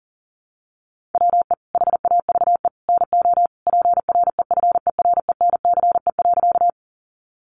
Morse code, heard as a keyed tone, about 40 wpm, 700 Hz. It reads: WE HAVE NO PREFERENCE.
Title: Sentences from Top 1500 Words 40wpm